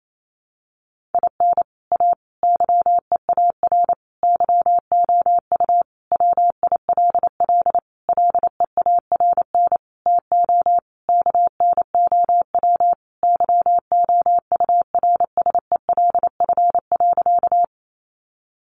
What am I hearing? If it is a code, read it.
Morse code, heard as a keyed tone, about 28 wpm, 700 Hz. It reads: IN A YEAR YOU WILL LEARN TO KNOW YOURSELF.
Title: Sentences from Top 600 Words 28wpm